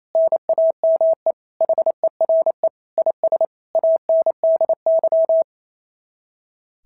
Morse code, heard as a keyed tone, about 28 wpm, 690 Hz. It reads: NAME HERE IS ANDY
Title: SOTA QSO Practice 28wpm